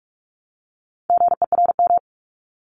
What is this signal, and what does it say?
Morse code, heard as a keyed tone, about 46 wpm, 700 Hz. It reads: GERM
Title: Multi-letter Phonemes Repeated 3x 45wpm